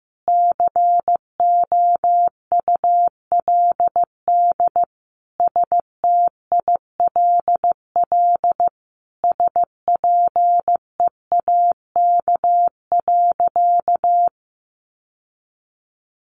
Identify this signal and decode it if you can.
Morse code, heard as a keyed tone, about 15 wpm, 700 Hz. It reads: COULD STILL SPEAK.
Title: Sentences from Top 1100 Words 15wpm